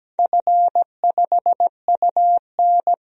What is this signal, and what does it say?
Morse code, heard as a keyed tone, about 17 wpm, 700 Hz. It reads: F5UN
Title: DXCC-100 Call Signs - 4 Characters - 17wpm